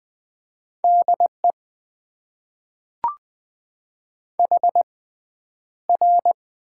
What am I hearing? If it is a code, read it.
Morse code, heard as a keyed tone, about 20 wpm, spaced as for 10 wpm, 700 Hz. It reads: DE HR
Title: Top 90 QSO Elements - 6x Character Spacing - 20wpm